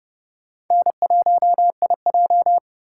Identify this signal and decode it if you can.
Morse code, heard as a keyed tone, about 31 wpm, 700 Hz. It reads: N1IJ
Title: USA Call Signs - 4x Character Spacing - 30wpm